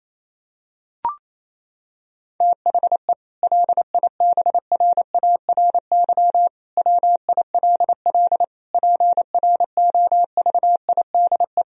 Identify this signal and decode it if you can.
Morse code, heard as a keyed tone, about 28 wpm, 700 Hz. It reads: THE LIBRARY WILL PROVIDE
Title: Sentences from Top 1000 Words - Review Part 4 - 28wpm